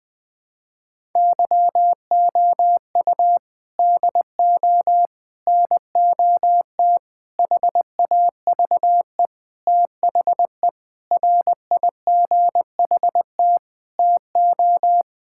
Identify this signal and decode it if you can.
Morse code, heard as a keyed tone, about 20 wpm, 700 Hz. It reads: YOU DO NOT HAVE THE RIGHT TO
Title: Sentences with 5-Word Phrases - 20wpm